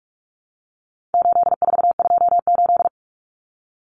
Morse code, heard as a keyed tone, about 44 wpm, 700 Hz. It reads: 8428
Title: Numbers - 4-Digit 45wpm